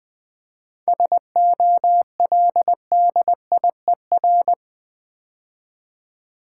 Morse code, heard as a keyed tone, about 20 wpm, 700 Hz. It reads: SOLDIER
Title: 7 Letter Words - First 3 Letters Given - 20wpm